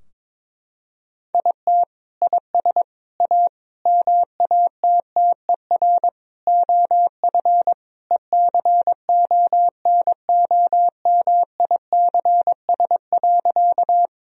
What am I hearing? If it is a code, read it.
Morse code, heard as a keyed tone, about 22 wpm, 700 Hz. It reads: IT IS A MATTER OF ECONOMICS.
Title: Sentences from Top 1000 Words - Review Part 3 - 22wpm